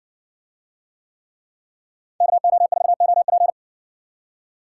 Morse code, heard as a keyed tone, about 60 wpm, 700 Hz. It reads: KO4OP